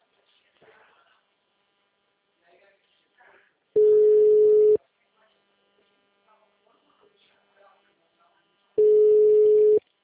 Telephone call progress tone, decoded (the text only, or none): ringback tone